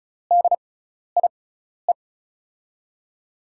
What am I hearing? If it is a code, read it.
Morse code, heard as a keyed tone, about 36 wpm, spaced as for 12 wpm, 700 Hz. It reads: DIE
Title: USA Call Signs - 6 Characters - 6x Character Spacing - 35wpm